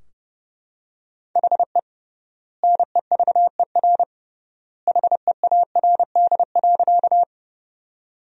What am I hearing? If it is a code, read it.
Morse code, heard as a keyed tone, about 32 wpm, 700 Hz. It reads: HE NEVER HEARD.
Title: 3x Word-Spacing - Sentences from Top 400 Words - 30wpm